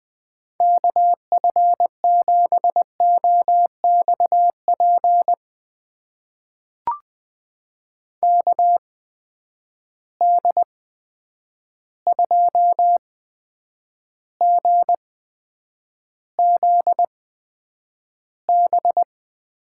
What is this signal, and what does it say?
Morse code, heard as a keyed tone, about 20 wpm, spaced as for 9 wpm, 700 Hz. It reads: KF7OXP KD2GZB
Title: USA Call Signs - 8x Character Spacing - 20wpm